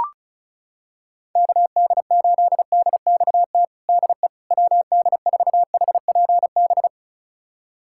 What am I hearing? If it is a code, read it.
Morse code, heard as a keyed tone, about 36 wpm, 700 Hz. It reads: KD8DXT DE WD4HPB
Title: USA Callsign DE Callsign - 6 Characters - 35wpm